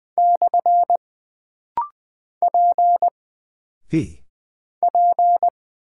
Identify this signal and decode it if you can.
Morse code, heard as a keyed tone, about 20 wpm, 700 Hz. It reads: / P P
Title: Single Character (Letter:Number:Punctuation:Prosign) - Rapid-Fire - 20wpm